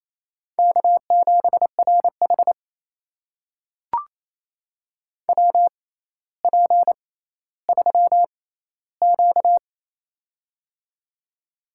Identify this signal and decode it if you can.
Morse code, heard as a keyed tone, about 28 wpm, spaced as for 14 wpm, 700 Hz. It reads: K7RH WP3Q
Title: USA Call Signs - 4 Characters - 6x Character Spacing - 28wpm